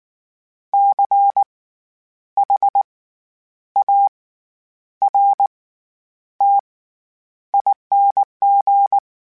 Morse code, heard as a keyed tone, about 19 wpm, spaced as for 9 wpm, 800 Hz. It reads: CHARTING
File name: Special INGs / Special INGs-mp3s / SlowToFaster ing @18